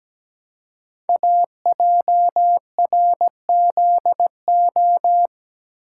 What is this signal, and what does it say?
Morse code, heard as a keyed tone, about 17 wpm, 700 Hz. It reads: AJRZO
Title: Code Groups - 5 Letter:Numbers - 17wpm